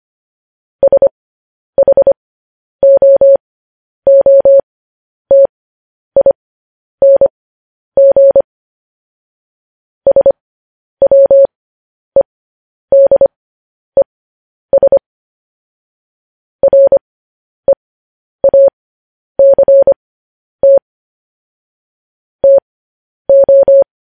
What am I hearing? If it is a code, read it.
Morse code, heard as a keyed tone, about 25 wpm, spaced as for 10 wpm, 550 Hz. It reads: SHOOTING SWEDES REACT TO
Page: news-headlines-in-morse-code-at-10-wpm